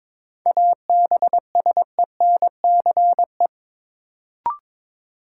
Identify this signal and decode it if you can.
Morse code, heard as a keyed tone, about 22 wpm, 700 Hz. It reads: ABSENCE
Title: Sets of 4 Words - New Word Drill - 22wpm